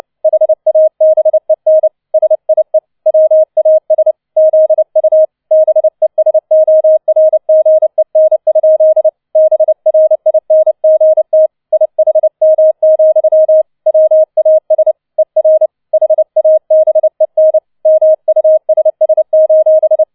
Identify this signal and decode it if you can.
Morse code, heard as a keyed tone, about 29 wpm, 630 Hz. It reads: HABEN SIE WAS ZU BESORGEN? BRINGT IHM, WAS ER HABEN MUSS: